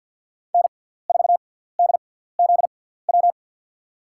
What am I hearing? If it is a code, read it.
Morse code, heard as a keyed tone, about 53 wpm, spaced as for 17 wpm, 700 Hz. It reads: N4DZW
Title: USA Call Signs - 5 Characters - 6x Character Spacing - 50wpm